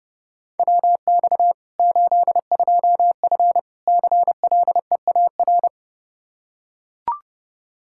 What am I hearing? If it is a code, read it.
Morse code, heard as a keyed tone, about 30 wpm, 700 Hz. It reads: WX 82F CLEAR
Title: Practice QSO - Exchange 2 - Info - 30wpm